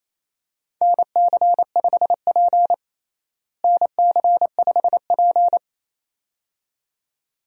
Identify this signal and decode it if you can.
Morse code, heard as a keyed tone, about 28 wpm, 700 Hz. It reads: NC5P NC5P
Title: USA Call Signs - 4 Characters - Repeated 2x - 28wpm